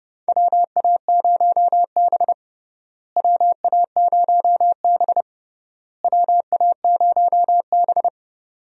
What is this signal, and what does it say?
Morse code, heard as a keyed tone, about 30 wpm, 700 Hz. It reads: WA0B WA0B WA0B
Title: USA Call Signs - 4 Characters - Repeated 3x - 30wpm